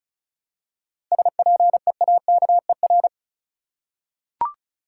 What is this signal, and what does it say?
Morse code, heard as a keyed tone, about 36 wpm, 700 Hz. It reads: SPEAKER
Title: Seven Letter Words - 35wpm